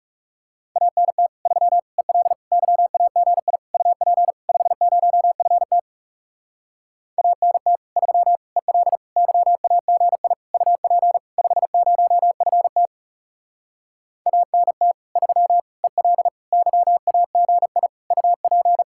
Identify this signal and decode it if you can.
Morse code, heard as a keyed tone, about 40 wpm, 700 Hz. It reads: ANT 3 EL YAGI UP 50FT ANT 3 EL YAGI UP 50FT ANT 3 EL YAGI UP